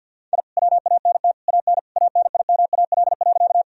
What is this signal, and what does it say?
Morse code, heard as a keyed tone, about 51 wpm, 700 Hz. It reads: I WANT AN ANIMAL.
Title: Sentences from Top 1000 Words - Review Part 1 - 50wpm